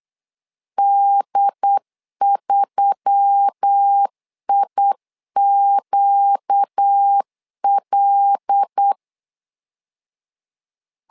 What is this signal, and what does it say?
Morse code, heard as a keyed tone, about 8 wpm, 800 Hz. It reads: D3IQL
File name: LEZIONI / 5 LBVC / 5 LBV 8wpm 2837